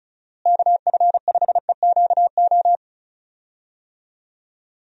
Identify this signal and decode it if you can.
Morse code, heard as a keyed tone, about 36 wpm, 700 Hz. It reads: KF5EQO
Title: USA Call Signs - 6 Characters - Repeated 2x - 35wpm